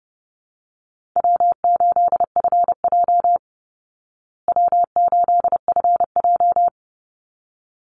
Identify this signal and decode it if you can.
Morse code, heard as a keyed tone, about 30 wpm, 700 Hz. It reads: W8FJ W8FJ
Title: Easy Call Signs Repeated 2x - 30wpm